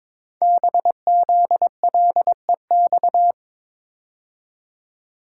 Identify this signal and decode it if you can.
Morse code, heard as a keyed tone, about 22 wpm, 700 Hz. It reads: BZLEX
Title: Code Groups - 5 Letters - 22wpm